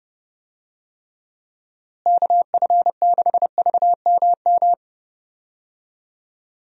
Morse code, heard as a keyed tone, about 31 wpm, 700 Hz. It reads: KF6VMM
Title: USA Call Signs - 6 Characters - Repeated 3x - 30wpm